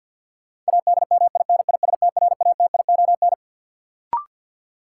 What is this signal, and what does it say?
Morse code, heard as a keyed tone, about 49 wpm, 700 Hz. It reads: ADMINISTRATION
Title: Sets of 5 Words - New Word Drill - 50wpm